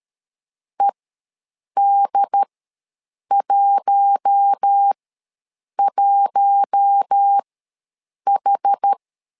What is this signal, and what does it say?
Morse code, heard as a keyed tone, about 13 wpm, spaced as for 7 wpm, 800 Hz. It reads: ED11H
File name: LEZIONI / 4 Q1 ws 12wpm 2129